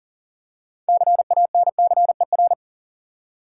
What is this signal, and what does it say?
Morse code, heard as a keyed tone, about 40 wpm, 700 Hz. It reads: CANCER
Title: Six Letter Words - 40wpm